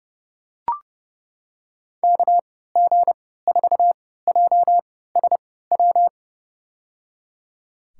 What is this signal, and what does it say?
Morse code, heard as a keyed tone, about 30 wpm, spaced as for 17 wpm, 700 Hz. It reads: KG4JSW
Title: USA Call Signs - 6 Characters - 3x Character Spacing - 30wpm